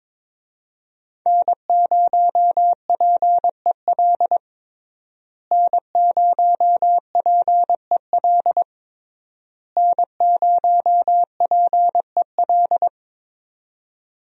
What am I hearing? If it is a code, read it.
Morse code, heard as a keyed tone, about 22 wpm, 700 Hz. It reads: N0PEL N0PEL N0PEL
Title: USA Call Signs - 5 Characters - Repeated 3x - 22wpm